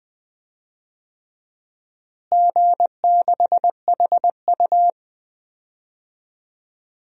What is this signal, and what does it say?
Morse code, heard as a keyed tone, about 20 wpm, 700 Hz. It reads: G6HU
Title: DXCC-100 Call Signs - 4 Characters - Repeated 3x - 20wpm